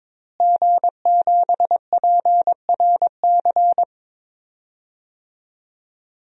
Morse code, heard as a keyed tone, about 22 wpm, 700 Hz. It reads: G7PRC
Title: DXCC-100 Call Signs - 5 Characters - Repeated 2x - 22wpm